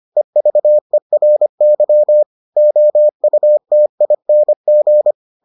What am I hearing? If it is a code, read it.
Morse code, heard as a keyed tone, about 25 wpm, 600 Hz. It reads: EVERY OUTING